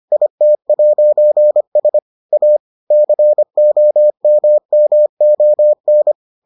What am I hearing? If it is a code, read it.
Morse code, heard as a keyed tone, about 25 wpm, 600 Hz. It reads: IT'S A COMMON